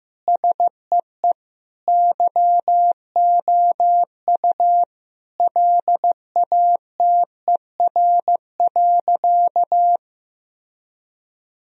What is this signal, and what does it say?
Morse code, heard as a keyed tone, about 15 wpm, 700 Hz. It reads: SEE YOU LATER.
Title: Sentences from Top 1000 Words - Review Part 4 - 15wpm